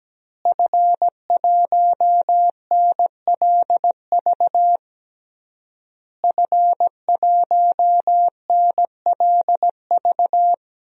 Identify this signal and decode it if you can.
Morse code, heard as a keyed tone, about 17 wpm, 700 Hz. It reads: F1NLV F1NLV
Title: DXCC-100 Call Signs - 5 Characters - Repeated 3x - 17wpm